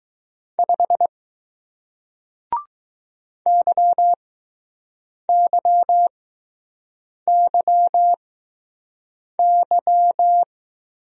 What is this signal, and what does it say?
Morse code, heard as a keyed tone, about 20 wpm, spaced as for 6 wpm, 700 Hz. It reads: 5 YYYY